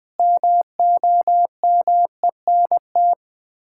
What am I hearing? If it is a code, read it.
Morse code, heard as a keyed tone, about 20 wpm, 700 Hz. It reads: MOMENT
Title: 2x Word-Spacing - Sets of 5 Words - 20wpm